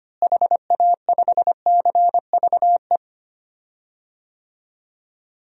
Morse code, heard as a keyed tone, about 25 wpm, 700 Hz. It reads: HA5CVE
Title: DXCC-100 Call Signs - 6 Characters - Repeated 2x - 25wpm